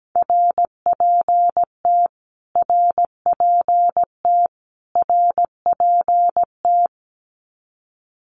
Morse code, heard as a keyed tone, about 17 wpm, 700 Hz. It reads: RPT RPT RPT
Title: Abbreviations Repeated 3x - 17wpm